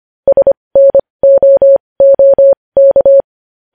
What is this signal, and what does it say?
Morse code, heard as a keyed tone, about 25 wpm, spaced as for 20 wpm, 550 Hz. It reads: SNOOK
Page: news-headlines-in-morse-code-at-20-wpm